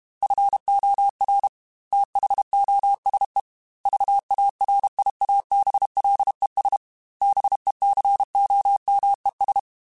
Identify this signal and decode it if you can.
Morse code, heard as a keyed tone, about 32 wpm, 800 Hz. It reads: FOR THOSE VARIABLES BECOMES
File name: Technical Subjects / Technical Subjects-mp3s / AND Gate @30